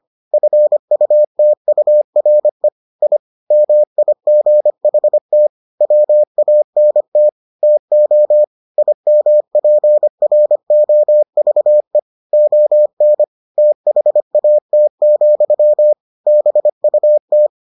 Morse code, heard as a keyed tone, about 25 wpm, 600 Hz. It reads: FUTURE I MIGHT WANT TO IMPROVE ON THAT, BUT